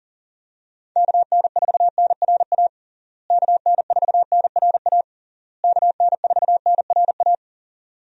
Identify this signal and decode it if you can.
Morse code, heard as a keyed tone, about 40 wpm, 700 Hz. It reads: KN4NRA KN4NRA KN4NRA
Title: USA Call Signs - 6 Characters - Repeated 3x - 40wpm